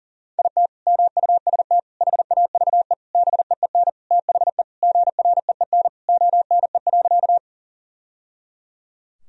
Morse code, heard as a keyed tone, about 40 wpm, 700 Hz. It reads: IT MUST HAVE BEEN THE GREEN ONE.
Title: Sentences from Top 1000 Words - Review Part 2 - 40wpm